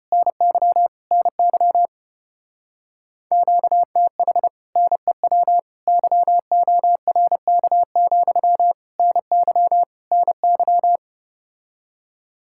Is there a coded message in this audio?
Yes: Morse code from 0.1 to 11.0 s, heard as a keyed tone, about 33 wpm, 700 Hz. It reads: NY NY QTH NEW YORK, NY NY